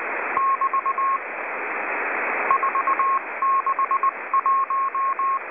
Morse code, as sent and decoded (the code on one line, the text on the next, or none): -...- / ....- -.... .----
<BT> 461